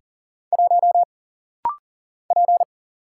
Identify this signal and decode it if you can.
Morse code, heard as a keyed tone, about 40 wpm, 700 Hz. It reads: 1 P
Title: Single Letter-Number - Rapid-Fire 40wpm